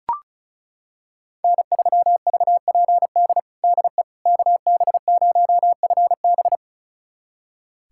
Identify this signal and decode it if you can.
Morse code, heard as a keyed tone, about 36 wpm, 700 Hz. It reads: N3VPD DE KB0FB
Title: USA Callsign DE Callsign - 5 Characters - 35wpm